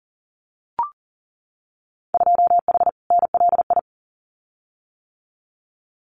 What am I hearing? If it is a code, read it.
Morse code, heard as a keyed tone, about 41 wpm, 700 Hz. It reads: 2H NLI
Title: Winter Field Day Exchanges 40wpm